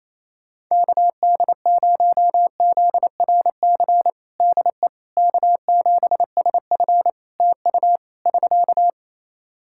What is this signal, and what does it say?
Morse code, heard as a keyed tone, about 28 wpm, 700 Hz. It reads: KD0ZRC DE K7SF TU <SK>
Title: Practice QSO - Exchange 4 - Ending - 28wpm